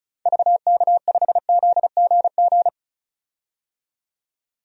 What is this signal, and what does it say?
Morse code, heard as a keyed tone, about 36 wpm, 700 Hz. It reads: VK5ZGG
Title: DXCC-100 Call Signs - 6 Characters - 35wpm